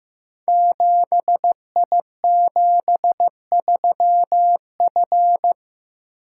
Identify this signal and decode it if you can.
Morse code, heard as a keyed tone, about 15 wpm, 700 Hz. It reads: 7I73F
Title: Code Groups - 5 Letter:Numbers - 15wpm